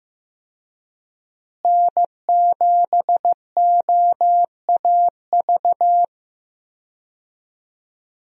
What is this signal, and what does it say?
Morse code, heard as a keyed tone, about 15 wpm, 700 Hz. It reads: N7OAV
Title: USA Call Signs - 5 Characters - Repeated 2x - 15wpm